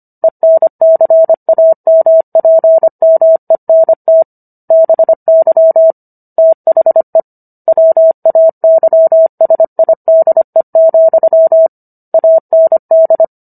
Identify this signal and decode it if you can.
Morse code, heard as a keyed tone, about 25 wpm, 650 Hz. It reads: ENCAMPMENT BY THE WAYSIDE, AND